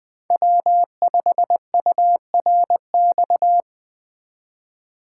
Morse code, heard as a keyed tone, about 21 wpm, 700 Hz. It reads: W5URX
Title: USA Call Signs - 5 Characters - 3x Character Spacing - 20wpm